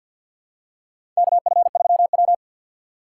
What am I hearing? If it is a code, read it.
Morse code, heard as a keyed tone, about 52 wpm, 700 Hz. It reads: KF3W
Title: USA Call Signs - 4 Characters - 6x Character Spacing - 50wpm